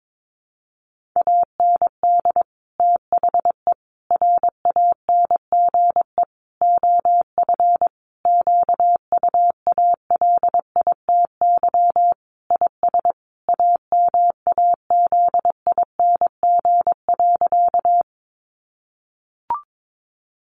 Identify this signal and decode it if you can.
Morse code, heard as a keyed tone, about 22 wpm, 700 Hz. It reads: AND THE RANGE OF QUALITY IS AMAZING.
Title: Sentences from Top 1500 Words 22wpm